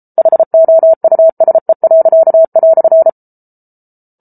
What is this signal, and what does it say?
Morse code, heard as a keyed tone, about 34 wpm, 650 Hz. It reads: HOUSE."